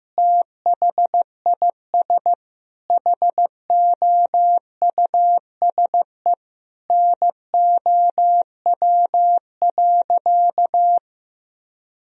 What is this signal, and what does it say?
Morse code, heard as a keyed tone, about 16 wpm, 700 Hz. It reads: THIS HOUSE NOW.
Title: Sentences from Top 1000 Words - Review Part 2 - 15wpm